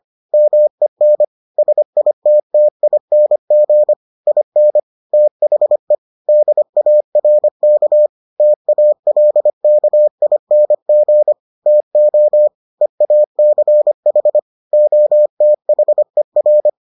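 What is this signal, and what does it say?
Morse code, heard as a keyed tone, about 25 wpm, 600 Hz. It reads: MEN SITTING IN THE DARK TALKING TO EACH OTHER
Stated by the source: Morse edition of the Foundations of Amateur Radio episodes.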